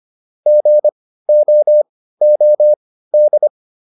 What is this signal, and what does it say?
Morse code, heard as a keyed tone, about 25 wpm, spaced as for 15 wpm, 600 Hz. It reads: GOOD